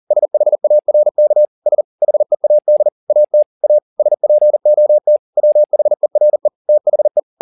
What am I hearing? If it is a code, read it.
Morse code, heard as a keyed tone, about 40 wpm, 600 Hz. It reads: SHARKS HEAD AT A SPOT WHERE THE
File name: the old man and the sea 2 40wpm